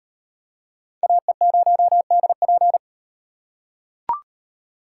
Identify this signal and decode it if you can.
Morse code, heard as a keyed tone, about 35 wpm, 700 Hz. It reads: AE0DP